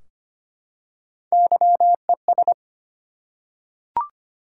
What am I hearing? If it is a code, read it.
Morse code, heard as a keyed tone, about 25 wpm, 700 Hz. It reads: YES
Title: Three Letter Words - 25wpm